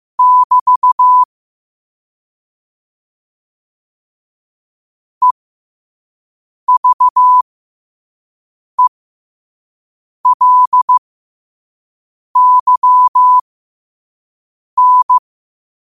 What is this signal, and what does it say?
Morse code, heard as a keyed tone, about 15 wpm, spaced as for 5 wpm, 1000 Hz. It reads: <BT> EVELYN
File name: qotdfiles / 5wpm